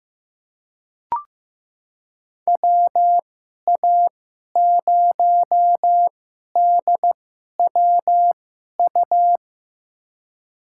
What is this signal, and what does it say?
Morse code, heard as a keyed tone, about 16 wpm, spaced as for 11 wpm, 700 Hz. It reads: WA0DWU
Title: USA Call Signs - 6 Characters - 2x Character Spacing - 15wpm